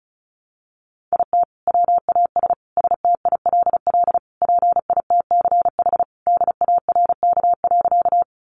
Morse code, heard as a keyed tone, about 34 wpm, 700 Hz. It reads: IT WAS STILL PITCH DARK.
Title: Sentences from Top 1300 Words 35wpm